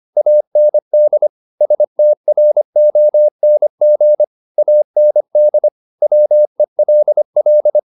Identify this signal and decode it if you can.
Morse code, heard as a keyed tone, about 25 wpm, 600 Hz. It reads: AND STRONG AND WELL